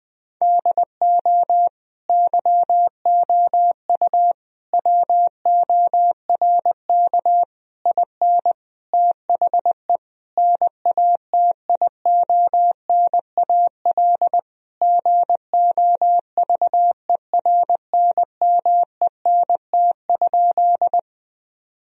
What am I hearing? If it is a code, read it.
Morse code, heard as a keyed tone, about 20 wpm, 700 Hz. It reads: DO YOU WORK IN THE NATIONAL GOVERNMENT?
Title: Sentences from Top 1000 Words - Review Part 4 - 20wpm